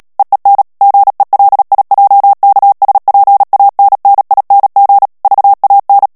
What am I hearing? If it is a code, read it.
Morse code, heard as a keyed tone, about 37 wpm, 800 Hz. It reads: EEN GELIJKSPANNING VAN
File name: morse mp3 / HOLLANDAIS texte clair 2 vitesse 35 mots minute